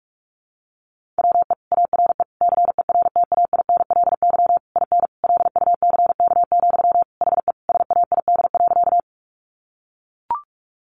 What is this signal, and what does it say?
Morse code, heard as a keyed tone, about 44 wpm, 700 Hz. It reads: WE ARE CERTAINLY IN LUCK, HE SAID.
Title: Sentences from Top 1500 Words 45wpm